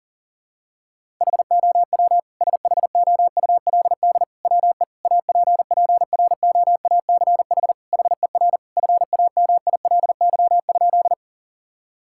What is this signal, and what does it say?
Morse code, heard as a keyed tone, about 40 wpm, 700 Hz. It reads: HOW SHOULD WE APPROACH HER FAMILY?
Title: Sentences from Top 1000 Words - Review Part 6 - 40wpm